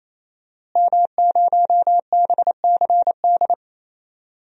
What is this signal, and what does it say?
Morse code, heard as a keyed tone, about 27 wpm, 700 Hz. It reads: M0BCD
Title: DXCC-100 Call Signs - 5 Characters - Repeated 3x - 28wpm